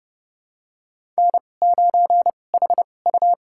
Morse code, heard as a keyed tone, about 29 wpm, spaced as for 22 wpm, 700 Hz. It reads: N9HU
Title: USA Call Signs - 2x Character Spacing - 30wpm